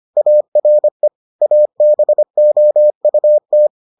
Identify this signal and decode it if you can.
Morse code, heard as a keyed tone, about 25 wpm, 600 Hz. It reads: ARE ABOUT